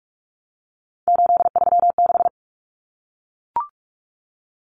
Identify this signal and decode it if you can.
Morse code, heard as a keyed tone, about 44 wpm, 700 Hz. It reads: 836
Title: Common Cut Numbers - 3-Digit 45wpm